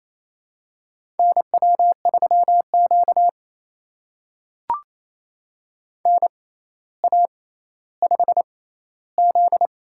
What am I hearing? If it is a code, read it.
Morse code, heard as a keyed tone, about 28 wpm, spaced as for 14 wpm, 700 Hz. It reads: NW3Q NA5Z
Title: USA Call Signs - 4 Characters - 6x Character Spacing - 28wpm